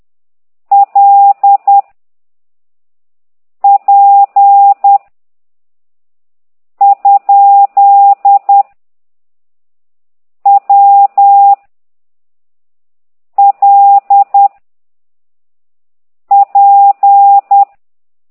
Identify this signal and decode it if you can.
Morse code, heard as a keyed tone, about 10 wpm, 800 Hz. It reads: L P ? W L P
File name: ct2idw / exerc8